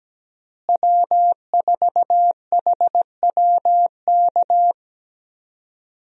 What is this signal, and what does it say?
Morse code, heard as a keyed tone, about 17 wpm, 700 Hz. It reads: W4HWK
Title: USA Call Signs - 5 Characters - Repeated 2x - 17wpm